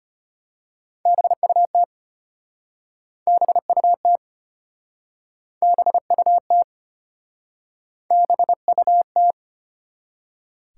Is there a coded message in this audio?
Yes: Morse code from 1.1 to 9.3 s, heard as a keyed tone, about 33 wpm, 700 Hz. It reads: BUT BUT BUT BUT